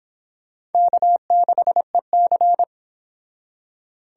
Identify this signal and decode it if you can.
Morse code, heard as a keyed tone, about 31 wpm, 700 Hz. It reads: K6EC